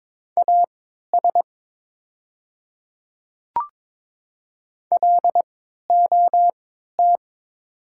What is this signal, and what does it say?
Morse code, heard as a keyed tone, about 22 wpm, spaced as for 12 wpm, 700 Hz. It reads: AS LOT
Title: 3x Character Spacing - Top 500 Words - 22wpm